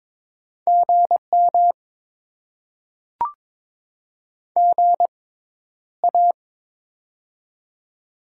Morse code, heard as a keyed tone, about 22 wpm, spaced as for 11 wpm, 700 Hz. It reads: GM GA
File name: Top 20 QSO Elements - 6x Character Spacing - 22wpm